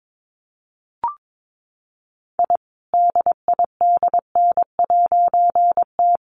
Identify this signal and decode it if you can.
Morse code, heard as a keyed tone, about 22 wpm, 700 Hz. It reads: I DIDN'T
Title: A Sound Of Thunder 22wpm